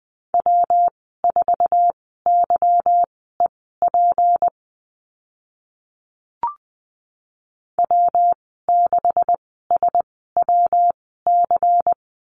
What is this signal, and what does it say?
Morse code, heard as a keyed tone, about 20 wpm, spaced as for 14 wpm, 700 Hz. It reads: W4YEP W6SWC
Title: 161 Lesson 26g - Letter g - Call Signs 20wpm